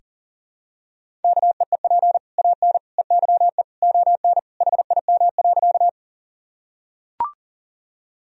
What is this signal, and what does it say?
Morse code, heard as a keyed tone, about 37 wpm, 700 Hz. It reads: KEEP AN EYE ON HIM.
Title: Sentences from Top 1000 Words - Review Part 1 - 40wpm